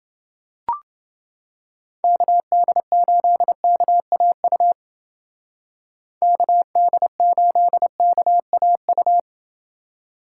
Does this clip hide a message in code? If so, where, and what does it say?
Yes: Morse code from 2.0 to 9.2 s, heard as a keyed tone, about 29 wpm, 700 Hz. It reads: KD8KAU KD8KAU